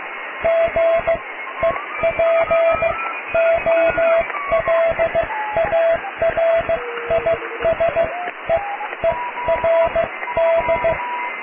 Morse code, heard as a keyed tone, about 15 wpm, spaced as for 12 wpm, 660 Hz, with radio noise. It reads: GEPOLARISEERD
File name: cw / tekst(a)+QRM